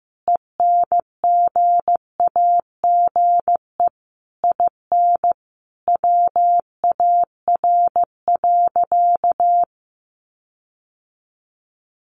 Morse code, heard as a keyed tone, about 15 wpm, 700 Hz. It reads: ENGAGE IN WAR.
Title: Sentences from Top 1500 Words 15wpm